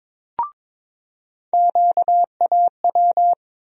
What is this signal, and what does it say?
Morse code, heard as a keyed tone, about 22 wpm, 700 Hz. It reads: QAW
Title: USA Call Sign Postfixes - 3 Letters 22wpm